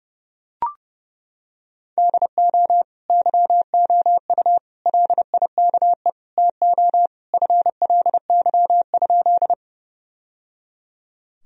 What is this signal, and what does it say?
Morse code, heard as a keyed tone, about 30 wpm, 700 Hz. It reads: DO YOU LIKE TO FLY?
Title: Sentences from Top 1000 Words - Review Part 2 - 30wpm